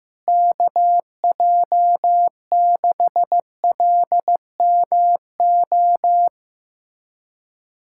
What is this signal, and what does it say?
Morse code, heard as a keyed tone, about 16 wpm, 700 Hz. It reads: KJ6LMO
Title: USA Callsign DE Callsign - 6 Characters - 15wpm